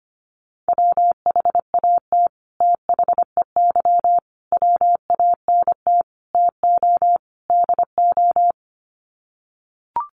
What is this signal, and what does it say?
Morse code, heard as a keyed tone, about 25 wpm, 700 Hz. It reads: WHAT THEY WANT TO DO
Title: Sets of 5 Words Encore2 25wpm